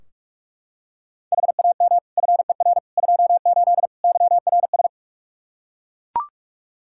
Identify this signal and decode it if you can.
Morse code, heard as a keyed tone, about 41 wpm, 700 Hz. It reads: HAM FER 28 YRS
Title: Practice QSO - Exchange 3 - More Info - 45wpm